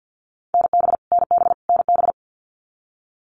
Morse code, heard as a keyed tone, about 50 wpm, 700 Hz. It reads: NB NB NB
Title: Canadian Provinces and Territories - Repeated 3x - 50wpm